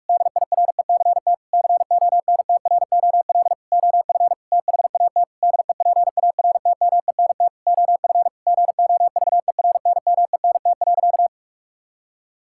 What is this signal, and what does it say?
Morse code, heard as a keyed tone, about 46 wpm, 700 Hz. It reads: DIRECT CONTROL OF THAT DEPARTMENT OF GOVERNMENT.
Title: Sentences from Top 1000 Words - Review Part 5 - 45wpm